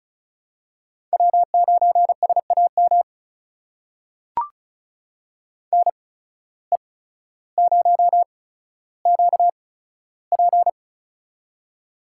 Morse code, heard as a keyed tone, about 34 wpm, spaced as for 15 wpm, 700 Hz. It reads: W9SAM NE0QP
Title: USA Call Signs - 5 Characters - 8x Character Spacing - 35wpm